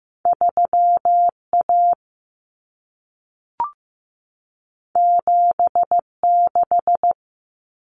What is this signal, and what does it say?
Morse code, heard as a keyed tone, about 15 wpm, 700 Hz. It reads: 3A 76
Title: Common Cut Numbers - 2-Digit 15wpm